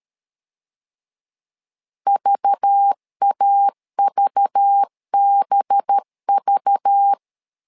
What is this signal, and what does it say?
Morse code, heard as a keyed tone, about 13 wpm, 800 Hz. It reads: VAVBV
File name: LEZIONI / 5 LBVC / 5 LBV 12wpm 1855